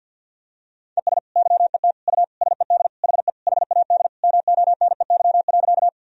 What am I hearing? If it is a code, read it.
Morse code, heard as a keyed tone, about 52 wpm, 700 Hz. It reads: ES YET U SED HE HAD MONEY.
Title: Sentences from Top 500 Words and Abbreviations 50wpm